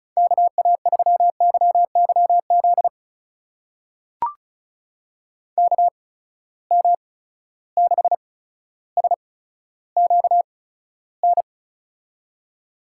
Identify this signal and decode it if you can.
Morse code, heard as a keyed tone, about 36 wpm, spaced as for 15 wpm, 700 Hz. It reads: KA3YYZ KM6SQN
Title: USA Call Signs - 6 Characters - 8x Character Spacing - 35wpm